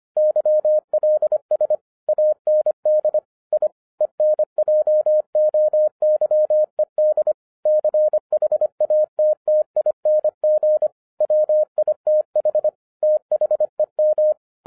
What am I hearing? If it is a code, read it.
Morse code, heard as a keyed tone, about 25 wpm, 610 Hz. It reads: YLS AND I ENJOYED CHATTING WITH THEM